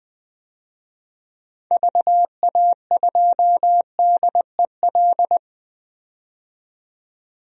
Morse code, heard as a keyed tone, about 20 wpm, 700 Hz. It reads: VA2DEL
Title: DXCC-100 Call Signs - 6 Characters - 20wpm